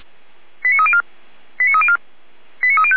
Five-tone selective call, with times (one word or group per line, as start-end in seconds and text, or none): 0.6-1.0 89304
1.6-2.0 89304
2.6-3.0 89305